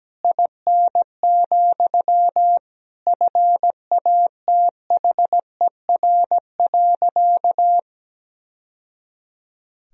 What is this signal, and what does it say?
Morse code, heard as a keyed tone, about 17 wpm, 700 Hz. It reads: IN, FATHER.
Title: Sentences from Top 1000 Words - Review Part 2 - 17wpm